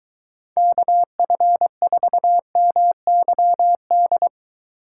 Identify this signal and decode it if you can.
Morse code, heard as a keyed tone, about 23 wpm, 700 Hz. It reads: KF4MYD